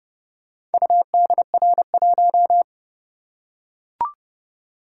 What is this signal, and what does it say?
Morse code, heard as a keyed tone, about 30 wpm, 700 Hz. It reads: UDR1